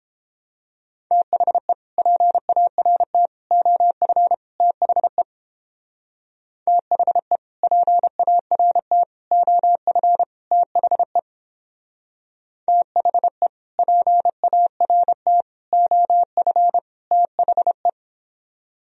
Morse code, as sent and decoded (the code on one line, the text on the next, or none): - .... / .--. .- .-. - / --- ..-. / - .... / - .... / .--. .- .-. - / --- ..-. / - .... / - .... / .--. .- .-. - / --- ..-. / - .... .
THE PART OF THE THE PART OF THE THE PART OF THE